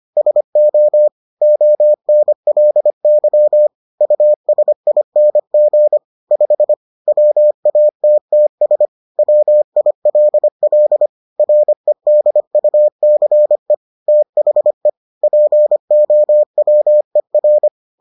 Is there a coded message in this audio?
Yes: Morse code from 0.2 to 17.7 s, heard as a keyed tone, about 25 wpm, 600 Hz. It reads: SO ONLY USING 5 WATTS WILL REDUCE THE POWER